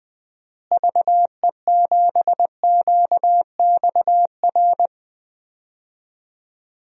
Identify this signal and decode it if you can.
Morse code, heard as a keyed tone, about 20 wpm, 700 Hz. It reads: VE7QXR
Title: DXCC-100 Call Signs - 6 Characters - Repeated 2x - 20wpm